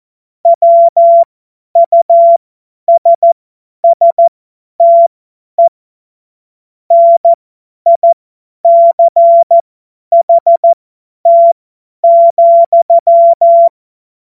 Morse code, heard as a keyed tone, about 14 wpm, spaced as for 10 wpm, 680 Hz. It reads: WUSSTE NICHT,
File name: Morsekurs / Witze / Witz 07. Dezember 2021, 50 F70